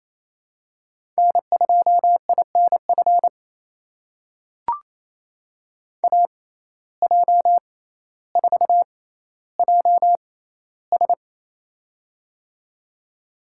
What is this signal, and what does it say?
Morse code, heard as a keyed tone, about 28 wpm, spaced as for 14 wpm, 700 Hz. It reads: N2INF AJ4JS
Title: USA Call Signs - 5 Characters - 6x Character Spacing - 28wpm